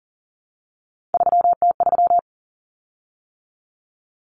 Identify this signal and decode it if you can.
Morse code, heard as a keyed tone, about 40 wpm, 700 Hz. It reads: 3T3
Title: Common Cut Numbers - 3-Digit 40wpm